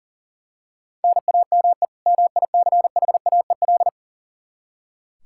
Morse code, heard as a keyed tone, about 37 wpm, 700 Hz. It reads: NAME MICHAEL
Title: Practice QSO - Exchange 1 - Beginning - 40wpm